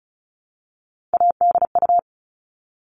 Morse code, heard as a keyed tone, about 34 wpm, 700 Hz. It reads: ADU
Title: All Cut Numbers - 3-Digit 35wpm